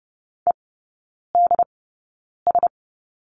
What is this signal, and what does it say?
Morse code, heard as a keyed tone, about 30 wpm, spaced as for 9 wpm, 700 Hz. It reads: EDS
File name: Lesson 16e - Letter c - Three Characters 30wpm